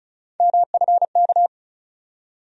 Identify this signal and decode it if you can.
Morse code, heard as a keyed tone, about 36 wpm, 700 Hz. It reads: MFK
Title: USA Call Sign Postfixes - 3 Letters 35wpm